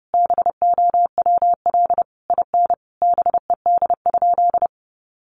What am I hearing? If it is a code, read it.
Morse code, heard as a keyed tone, about 30 wpm, 700 Hz. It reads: BOWL IN BED?
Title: Sentences from Top 1200 Words 30wpm